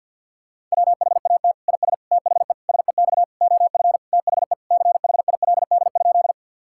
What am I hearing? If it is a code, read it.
Morse code, heard as a keyed tone, about 50 wpm, 700 Hz. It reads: WHAT IS THE SEX OF THE CHILD?
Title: Sentences from Top 1600 Words 50wpm